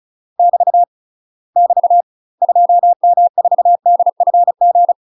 Morse code, heard as a keyed tone, about 34 wpm, 700 Hz. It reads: <BT> <BT> 2M4DFG
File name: UK MASTER QSOs TRAINING / 35WPM / UK MASTER QSOs 0002